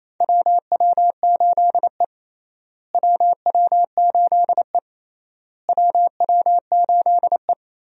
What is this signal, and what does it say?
Morse code, heard as a keyed tone, about 28 wpm, 700 Hz. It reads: WW8E WW8E WW8E
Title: USA Call Signs - 4 Characters - Repeated 3x - 28wpm